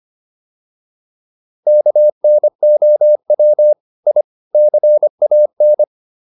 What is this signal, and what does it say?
Morse code, heard as a keyed tone, about 25 wpm, 600 Hz. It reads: KNOW I CAN